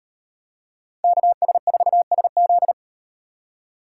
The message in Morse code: KS4SZ

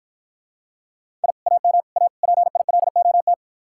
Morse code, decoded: I AM A PILOT

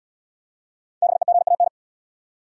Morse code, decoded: BEGAN

